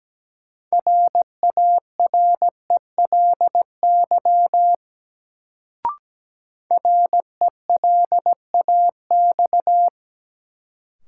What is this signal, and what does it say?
Morse code, heard as a keyed tone, about 17 wpm, 700 Hz. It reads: RARELY RELAX
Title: Sentences from Top 1400 Words - New Word Drill - 17wpm